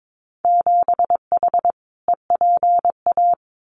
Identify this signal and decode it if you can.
Morse code, heard as a keyed tone, about 22 wpm, 700 Hz. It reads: 7H EPA
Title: Winter Field Day Exchanges Encore 22wpm